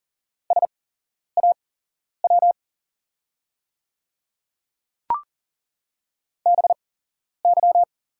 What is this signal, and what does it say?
Morse code, heard as a keyed tone, about 40 wpm, spaced as for 11 wpm, 690 Hz. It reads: SAW BY
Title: 8x Character Spacing - Top 300 Words - 40wpm